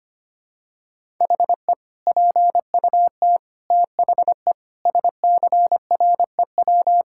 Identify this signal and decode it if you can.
Morse code, heard as a keyed tone, about 25 wpm, 700 Hz. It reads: HE PUT THE SCREW
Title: Sentences from Top 1600 Words 25wpm